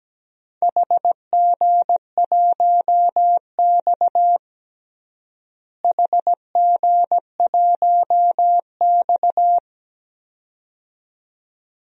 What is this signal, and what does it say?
Morse code, heard as a keyed tone, about 17 wpm, 700 Hz. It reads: HG1X HG1X
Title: DXCC-100 Call Signs - 4 Characters - Repeated 2x - 17wpm